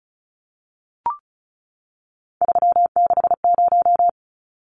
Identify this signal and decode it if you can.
Morse code, heard as a keyed tone, about 34 wpm, 700 Hz. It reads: 360
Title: Numbers - 3-Digit 35wpm